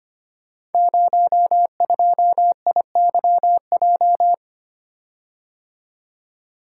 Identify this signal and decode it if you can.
Morse code, heard as a keyed tone, about 25 wpm, 700 Hz. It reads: 02IYJ
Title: Code Groups - 5 Letter:Numbers - 25wpm